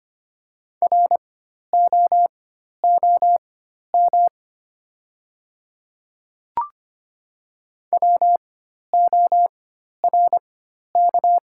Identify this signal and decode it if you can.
Morse code, heard as a keyed tone, about 25 wpm, spaced as for 12 wpm, 700 Hz. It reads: ROOM WORK
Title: 4x Character Spacing - Top 500 Words - 25wpm